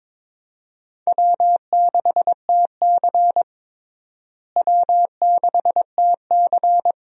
Morse code, heard as a keyed tone, about 22 wpm, 700 Hz. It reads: W6TC W6TC
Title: USA Call Signs - 4 Characters - Repeated 2x - 22wpm